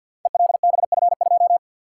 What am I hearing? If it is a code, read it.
Morse code, heard as a keyed tone, about 53 wpm, 700 Hz. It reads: ELBF2
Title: Code Groups - 5 Letter:Numbers - 50wpm